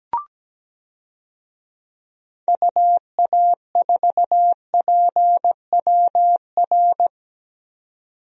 Morse code, heard as a keyed tone, about 17 wpm, 700 Hz. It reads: UA4PWR
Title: DXCC-100 Call Signs - 6 Characters - 17wpm